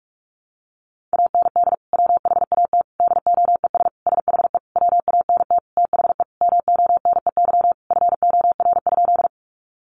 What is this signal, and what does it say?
Morse code, heard as a keyed tone, about 44 wpm, 700 Hz. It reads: AND WHAT DOES SHE WANT THE MONEY FOR?
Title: Sentences from Top 400 Words 45wpm